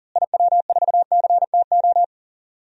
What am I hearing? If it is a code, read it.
Morse code, heard as a keyed tone, about 40 wpm, 700 Hz. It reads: IW4CTO
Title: DXCC-100 Call Signs - 6 Characters - Repeated 3x - 40wpm